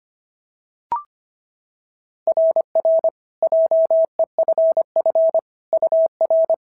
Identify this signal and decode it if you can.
Morse code, heard as a keyed tone, about 25 wpm, 690 Hz. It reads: RR JEFF UR
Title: SOTA QSO Practice 25wpm